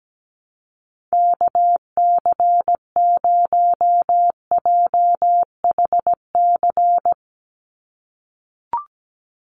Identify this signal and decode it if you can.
Morse code, heard as a keyed tone, about 17 wpm, 700 Hz. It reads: KC0JHC
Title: USA Call Signs 17wpm